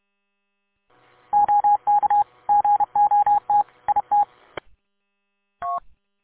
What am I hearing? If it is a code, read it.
Morse code, heard as a keyed tone, about 31 wpm, 820 Hz. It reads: OK GOT IT
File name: k8fbi-43732-combined